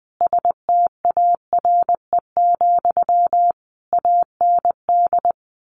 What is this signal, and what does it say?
Morse code, heard as a keyed tone, about 20 wpm, 700 Hz. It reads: STARE, AND
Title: A Sound Of Thunder 20wpm